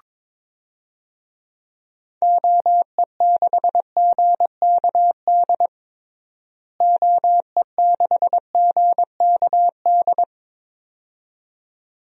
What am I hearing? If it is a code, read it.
Morse code, heard as a keyed tone, about 22 wpm, 700 Hz. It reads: OE6GKD OE6GKD
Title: DXCC-100 Call Signs - 6 Characters - Repeated 2x - 22wpm